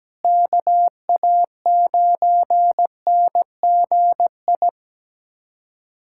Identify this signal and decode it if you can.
Morse code, heard as a keyed tone, about 17 wpm, 700 Hz. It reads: KA9NGI
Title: USA Callsign DE Callsign 17wpm